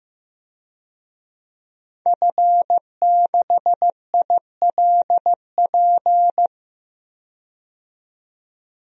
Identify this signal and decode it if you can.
Morse code, heard as a keyed tone, about 15 wpm, 700 Hz. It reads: F6ILP
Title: DXCC-100 Call Signs - 5 Characters - 15wpm